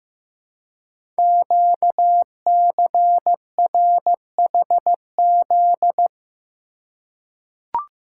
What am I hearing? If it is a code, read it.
Morse code, heard as a keyed tone, about 15 wpm, 700 Hz. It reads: QCRHZ
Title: Code Groups - 5 Letters - 15wpm